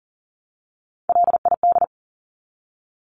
Morse code, heard as a keyed tone, about 40 wpm, 700 Hz. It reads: LID
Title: Abbreviations Repeated 3x - 40wpm